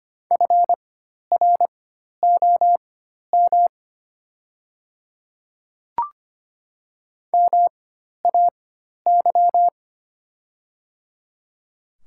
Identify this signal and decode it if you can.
Morse code, heard as a keyed tone, about 25 wpm, spaced as for 12 wpm, 700 Hz. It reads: FROM MAY
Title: 4x Character Spacing - Top 500 Words - 25wpm